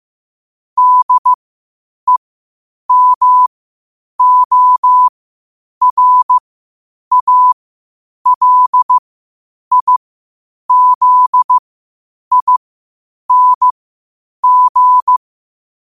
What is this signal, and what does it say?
Morse code, heard as a keyed tone, about 15 wpm, spaced as for 8 wpm, 1000 Hz. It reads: DEMORALIZING
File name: qotdfiles / 8wpm